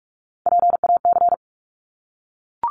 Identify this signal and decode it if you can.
Morse code, heard as a keyed tone, about 46 wpm, 700 Hz. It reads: PAC
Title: ARRL Sections 45wpm